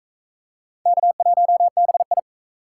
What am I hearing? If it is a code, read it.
Morse code, heard as a keyed tone, about 38 wpm, 700 Hz. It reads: K1BI